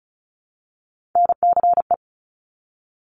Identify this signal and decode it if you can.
Morse code, heard as a keyed tone, about 36 wpm, 700 Hz. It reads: NCE
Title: Top Three Letter Combinations in Words 35wpm